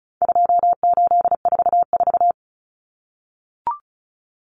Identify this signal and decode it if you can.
Morse code, heard as a keyed tone, about 34 wpm, 700 Hz. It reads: 2844
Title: Numbers - 4-Digit 35wpm